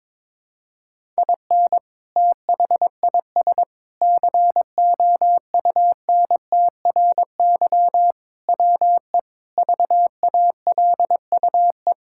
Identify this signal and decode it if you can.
Morse code, heard as a keyed tone, about 22 wpm, 700 Hz. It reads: IN THIS COUNTRY WE VALUE
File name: practice / 22wpm / Sentences with 3-Word Phrases - 22wpm